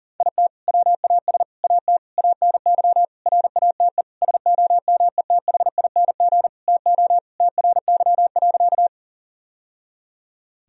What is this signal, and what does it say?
Morse code, heard as a keyed tone, about 40 wpm, 700 Hz. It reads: IT WAS AT ANY RATE SOMETHING TO TRY.
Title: Sentences from Top 1000 Words - Review Part 4 - 40wpm